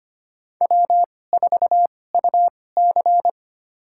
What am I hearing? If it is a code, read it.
Morse code, heard as a keyed tone, about 25 wpm, spaced as for 18 wpm, 700 Hz. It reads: W4UC
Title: USA Call Signs - 4 Characters - 2x Character Spacing - 25wpm